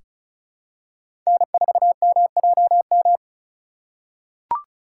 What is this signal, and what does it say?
Morse code, heard as a keyed tone, about 36 wpm, 700 Hz. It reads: N4MJM
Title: USA Call Signs - 5 Characters - 3x Character Spacing - 35wpm